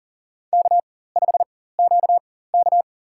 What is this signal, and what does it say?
Morse code, heard as a keyed tone, about 40 wpm, spaced as for 19 wpm, 700 Hz. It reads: K5QK
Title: USA Call Signs - 4 Characters - 4x Character Spacing - 40wpm